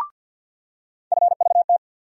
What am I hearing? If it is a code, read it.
Morse code, heard as a keyed tone, about 49 wpm, 700 Hz. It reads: FVT
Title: USA Call Sign Postfixes - 3 Letters 50wpm